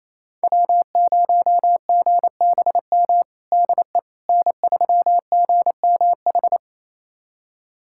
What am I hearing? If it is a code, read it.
Morse code, heard as a keyed tone, about 28 wpm, 700 Hz. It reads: W0GBM DE N3GMH
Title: USA Callsign DE Callsign - 5 Characters - 28wpm